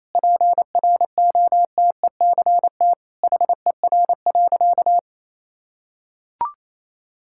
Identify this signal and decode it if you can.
Morse code, heard as a keyed tone, about 28 wpm, 700 Hz. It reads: PROTECT HER.
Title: Sentences from Top 1600 Words 28wpm